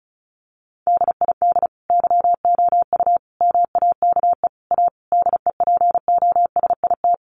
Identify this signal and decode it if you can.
Morse code, heard as a keyed tone, about 34 wpm, 700 Hz. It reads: DID YOU MAKE A DEPOSIT
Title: Sentences from Top 1300 Words 35wpm